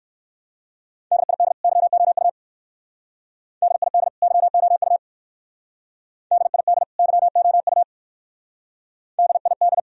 Morse code, heard as a keyed tone, about 58 wpm, 700 Hz. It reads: DID YOU DID YOU DID YOU DID